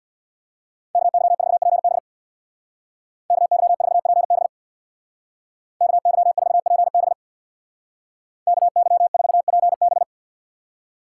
Morse code, heard as a keyed tone, about 61 wpm, 700 Hz. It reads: KY4PB KY4PB KY4PB KY4PB
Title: Speed Racing - USA 2x2 Call Signs 50wpm